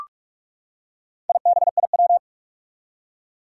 Morse code, heard as a keyed tone, about 44 wpm, 700 Hz. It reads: IBIW
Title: Four Challenging Letter-Number Combinations 45wpm